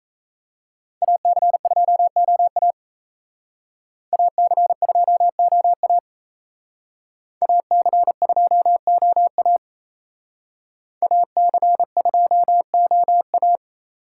Morse code, heard as a keyed tone, about 34 wpm, 700 Hz. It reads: AC2OA AC2OA AC2OA AC2OA
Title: Speed Racing - USA Call Signs 28wpm